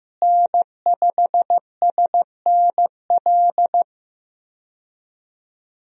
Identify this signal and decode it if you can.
Morse code, heard as a keyed tone, about 15 wpm, 700 Hz. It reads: N5SNL
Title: USA Call Signs - 5 Characters - Repeated 2x - 15wpm